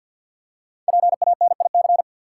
Morse code, heard as a keyed tone, about 49 wpm, 700 Hz. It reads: PANIC
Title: Five Letter Words - 50wpm